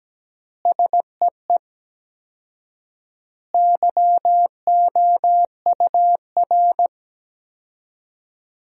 Morse code, heard as a keyed tone, about 17 wpm, 700 Hz. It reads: SEE YOUR
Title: 4x Word-Spacing - Sentences from Top 500 Words - 17wpm